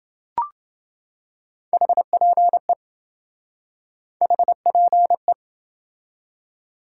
Morse code, heard as a keyed tone, about 26 wpm, 700 Hz. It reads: HPE HPE